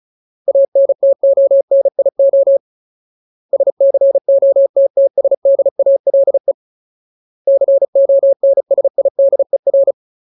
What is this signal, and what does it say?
Morse code, heard as a keyed tone, about 36 wpm, 550 Hz. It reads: ANTONIO SCOTTSDALE CONSIDER
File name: Ultimate Word List / 35WPM Ultimate